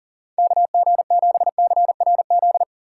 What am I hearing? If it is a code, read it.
Morse code, heard as a keyed tone, about 40 wpm, 690 Hz. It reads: KG7CRZ
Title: USA Call Signs - 3x Character Spacing - 40wpm